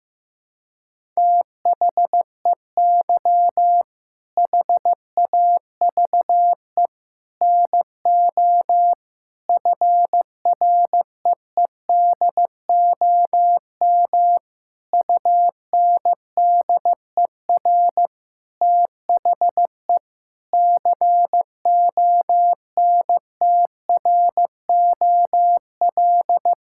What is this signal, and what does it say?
Morse code, heard as a keyed tone, about 15 wpm, 700 Hz. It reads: THEY HAVE NO FREEDOM UNDER THE CONTROL
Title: Sentences from Top 1000 Words - Review Part 5 - 15wpm